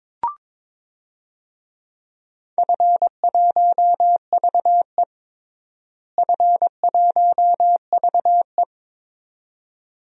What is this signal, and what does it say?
Morse code, heard as a keyed tone, about 22 wpm, 700 Hz. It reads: F1VE F1VE
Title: DXCC-100 Call Signs - 4 Characters - Repeated 2x - 22wpm